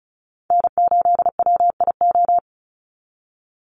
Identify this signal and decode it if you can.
Morse code, heard as a keyed tone, about 34 wpm, 700 Hz. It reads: N8WIO
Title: USA Call Signs 35wpm